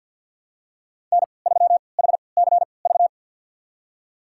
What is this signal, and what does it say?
Morse code, heard as a keyed tone, about 52 wpm, spaced as for 29 wpm, 700 Hz. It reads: N3HCV
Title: USA Call Signs - 5 Characters - 3x Character Spacing - 50wpm